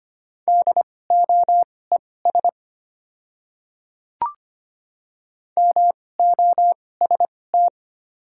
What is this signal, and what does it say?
Morse code, heard as a keyed tone, about 25 wpm, spaced as for 18 wpm, 690 Hz. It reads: DOES MOST
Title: 2x Character Spacing - Top 200 Words - 25wpm